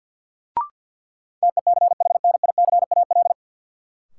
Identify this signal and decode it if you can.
Morse code, heard as a keyed tone, about 53 wpm, 700 Hz. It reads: TECHNICAL
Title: Sentences from Top 1100 Words - New Word Drill Alt - 50wpm